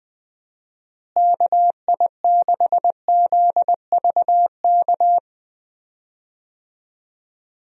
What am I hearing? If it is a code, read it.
Morse code, heard as a keyed tone, about 21 wpm, 700 Hz. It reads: KI6ZVK
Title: USA Call Signs - 6 Characters - Repeated 3x - 20wpm